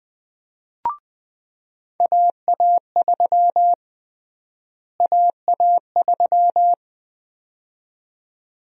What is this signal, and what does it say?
Morse code, heard as a keyed tone, about 20 wpm, 700 Hz. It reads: AA3 AA3
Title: USA Call Sign Prefixes - Repeated 2x - 20wpm